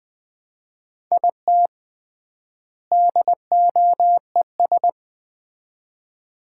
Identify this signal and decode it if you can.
Morse code, heard as a keyed tone, about 20 wpm, 700 Hz. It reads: IT DOES
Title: 3x Word-Spacing - Sets of 2 Words - 20wpm